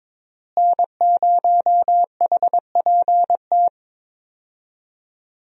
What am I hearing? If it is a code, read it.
Morse code, heard as a keyed tone, about 22 wpm, 700 Hz. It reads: N0HPT
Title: USA Call Signs - 5 Characters - Repeated 3x - 22wpm